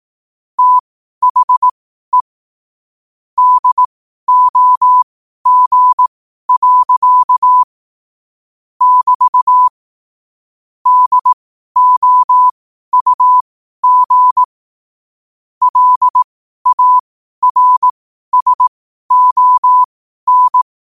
Morse code, heard as a keyed tone, about 18 wpm, spaced as for 12 wpm, 1000 Hz. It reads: THE DOG. <BT> DOUG LARSON
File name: qotdfiles / 12wpm